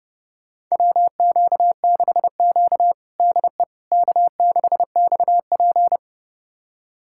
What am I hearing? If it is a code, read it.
Morse code, heard as a keyed tone, about 30 wpm, 700 Hz. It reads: WQ6Q DE K6XP
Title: USA Callsign DE Callsign - 4 Characters - 30wpm